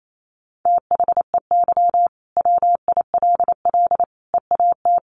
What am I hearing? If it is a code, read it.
Morse code, heard as a keyed tone, about 28 wpm, 700 Hz. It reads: THEY WILL EAT
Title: Sentences from Top 1000 Words 28wpm